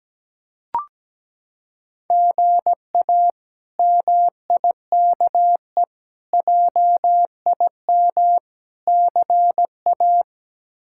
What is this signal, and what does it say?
Morse code, heard as a keyed tone, about 17 wpm, 700 Hz. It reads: GA MIKE JIM CA
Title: K1USN SST QSO Practice - 17wpm